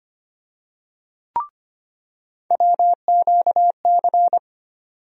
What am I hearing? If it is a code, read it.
Morse code, heard as a keyed tone, about 25 wpm, 700 Hz. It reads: WQC
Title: USA Call Sign Postfixes - 3 Letters 25wpm